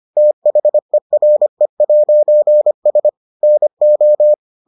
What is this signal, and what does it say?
Morse code, heard as a keyed tone, about 25 wpm, 600 Hz. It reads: THERE'S NO